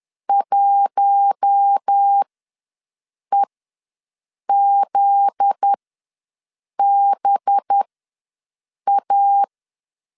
Morse code, heard as a keyed tone, about 10 wpm, spaced as for 6 wpm, 800 Hz. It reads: 1EZBA
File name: LEZIONI / 7 GF9Z / 7 GF9Z ws 10wpm 3919